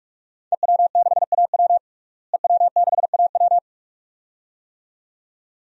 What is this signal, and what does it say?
Morse code, heard as a keyed tone, about 42 wpm, 700 Hz. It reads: EW6AW EW6AW
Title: DXCC-100 Call Signs - 5 Characters - Repeated 2x - 45wpm